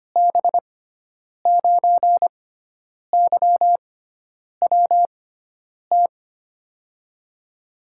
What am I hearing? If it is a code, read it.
Morse code, heard as a keyed tone, about 25 wpm, spaced as for 9 wpm, 700 Hz. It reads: B9YWT
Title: USA Call Signs - 6 Characters - 6x Character Spacing - 25wpm